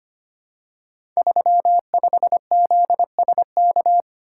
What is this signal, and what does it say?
Morse code, heard as a keyed tone, about 25 wpm, 700 Hz. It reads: 35ZSK
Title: Code Groups - 5 Letter:Numbers - 25wpm